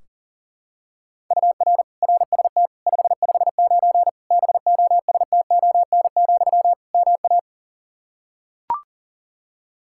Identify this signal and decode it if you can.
Morse code, heard as a keyed tone, about 40 wpm, 700 Hz. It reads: UR RST 559 BOSTON, MA
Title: Practice QSO - Exchange 1 - Beginning - 40wpm